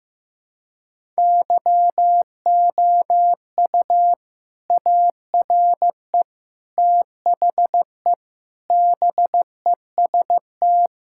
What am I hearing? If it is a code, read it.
Morse code, heard as a keyed tone, about 15 wpm, 700 Hz. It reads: YOU ARE THE BEST
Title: Sentences from Top 1000 Words - Review Part 6 - 15wpm